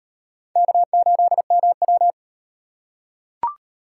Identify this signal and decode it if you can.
Morse code, heard as a keyed tone, about 38 wpm, 700 Hz. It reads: K8MW